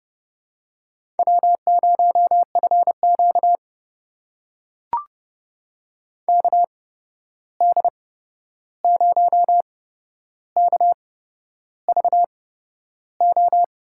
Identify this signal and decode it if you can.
Morse code, heard as a keyed tone, about 31 wpm, spaced as for 11 wpm, 700 Hz. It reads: W0FQ KD0KVO
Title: USA Call Signs - 8x Character Spacing - 30wpm